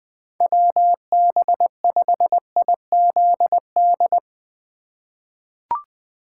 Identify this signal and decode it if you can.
Morse code, heard as a keyed tone, about 20 wpm, 700 Hz. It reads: WB5IZD
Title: USA Call Signs - 6 Characters - 3x Character Spacing - 20wpm